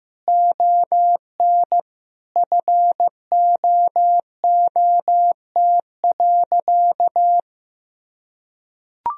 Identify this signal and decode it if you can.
Morse code, heard as a keyed tone, about 15 wpm, 700 Hz. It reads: ON FOOT.
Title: Sentences from Top 1000 Words - Review Part 3 - 15wpm